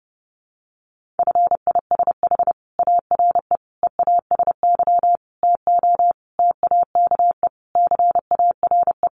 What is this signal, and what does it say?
Morse code, heard as a keyed tone, about 30 wpm, 700 Hz. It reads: FISH ARE EASY TO TAKE CARE
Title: Easy Phrases 30wpm